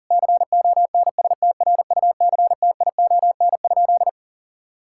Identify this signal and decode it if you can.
Morse code, heard as a keyed tone, about 37 wpm, 700 Hz. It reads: CONSTRUCTION?
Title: Sentences from Top 1600 Words 40wpm